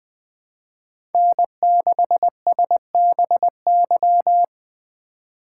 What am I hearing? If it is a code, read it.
Morse code, heard as a keyed tone, about 20 wpm, 700 Hz. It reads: N6SBY
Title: USA Call Signs - 5 Characters - 2x Character Spacing - 20wpm